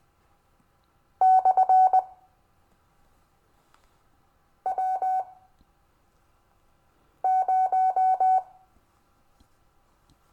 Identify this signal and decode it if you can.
Morse code, heard as a keyed tone, about 20 wpm, 1500 Hz. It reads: / W 0
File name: cw / Lesson15 five forwardslash